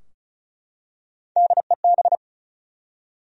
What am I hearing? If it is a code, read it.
Morse code, heard as a keyed tone, about 36 wpm, 700 Hz. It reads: DEB
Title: USA Common Names - Max 4 Letters - 35wpm